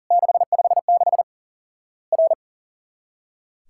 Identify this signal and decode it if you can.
Morse code, heard as a keyed tone, about 42 wpm, 700 Hz. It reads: 656 R
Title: Worked All Europe DX Contest - QTC Exchange - 40wpm